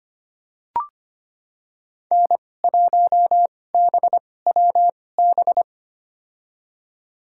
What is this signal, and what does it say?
Morse code, heard as a keyed tone, about 25 wpm, spaced as for 18 wpm, 700 Hz. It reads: N1BWB
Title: USA Call Signs - 5 Characters - 2x Character Spacing - 25wpm